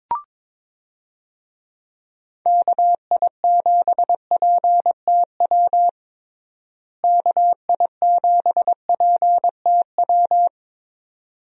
Morse code, heard as a keyed tone, about 22 wpm, 700 Hz. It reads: KI7PTW KI7PTW
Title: USA Call Signs - 6 Characters - Repeated 2x - 22wpm